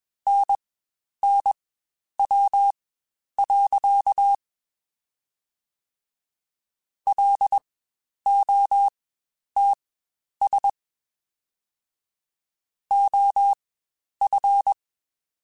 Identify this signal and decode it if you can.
Morse code, heard as a keyed tone, about 21 wpm, spaced as for 10 wpm, 800 Hz. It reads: NNW. LOTS OF
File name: Spaced Characters / Spaced Characters-mp3s / Weather 5 12 sp @20